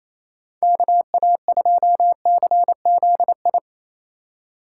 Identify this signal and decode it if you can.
Morse code, heard as a keyed tone, about 30 wpm, 700 Hz. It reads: KA2CZI
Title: Speed Racing - USA 2x3 Call Signs 28wpm